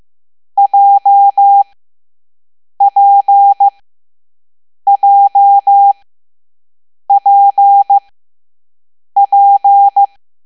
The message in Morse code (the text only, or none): JPJPP